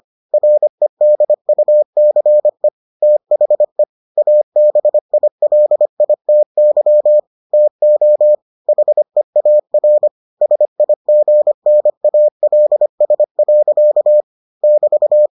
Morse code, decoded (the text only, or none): REDUCE THE ABILITY TO HEAR SIGNALS. <BT>